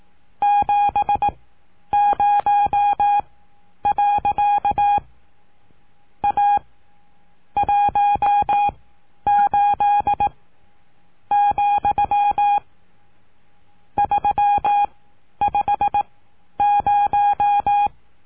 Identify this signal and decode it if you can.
Morse code, heard as a keyed tone, about 18 wpm, spaced as for 9 wpm, 820 Hz. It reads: 70. A 18,350
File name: cursos / CURSO DE TELEGRAFIA / Telegrafia - Trilha 19